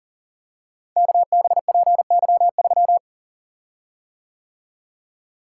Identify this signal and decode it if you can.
Morse code, heard as a keyed tone, about 40 wpm, 700 Hz. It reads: KBPY3
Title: Code Groups - 5 Letter:Numbers - 40wpm